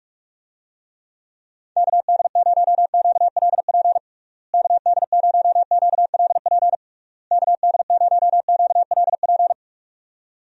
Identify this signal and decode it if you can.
Morse code, heard as a keyed tone, about 46 wpm, 700 Hz. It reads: KD0QLP KD0QLP KD0QLP
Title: USA Call Signs - 6 Characters - Repeated 3x - 45wpm